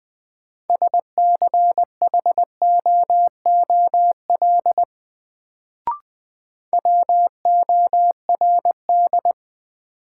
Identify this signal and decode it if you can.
Morse code, heard as a keyed tone, about 20 wpm, 700 Hz. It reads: SCHOOL WORD
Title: Sentences from Top 300 Words - New Word Drill - 20wpm